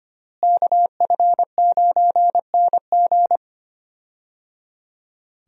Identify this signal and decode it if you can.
Morse code, heard as a keyed tone, about 25 wpm, 700 Hz. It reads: KF9NG
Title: USA Call Signs - 5 Characters - Repeated 3x - 25wpm